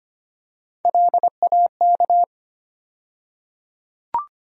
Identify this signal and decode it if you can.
Morse code, heard as a keyed tone, about 25 wpm, 700 Hz. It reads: LAK
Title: USA Call Sign Postfixes - 3 Letters 25wpm